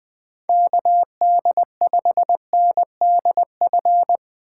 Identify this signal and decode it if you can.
Morse code, heard as a keyed tone, about 20 wpm, 700 Hz. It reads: KD5NDF
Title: USA Call Signs - 6 Characters - 3x Character Spacing - 20wpm